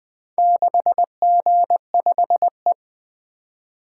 Morse code, heard as a keyed tone, about 24 wpm, 700 Hz. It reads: 6G5E
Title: Speed-Racing - Four Letter-Number Combinations - 20wpm